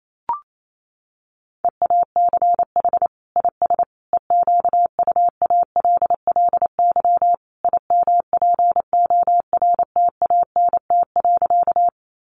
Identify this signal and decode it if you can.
Morse code, heard as a keyed tone, about 28 wpm, 700 Hz. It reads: EACH IS EQUALLY IMPORTANT.
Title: Sentences from Top 1400 Words 28wpm